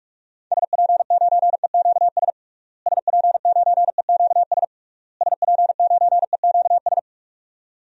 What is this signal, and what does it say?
Morse code, heard as a keyed tone, about 41 wpm, 700 Hz. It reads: SP9EQS SP9EQS SP9EQS
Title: DXCC-100 Call Signs - 6 Characters - Repeated 3x - 45wpm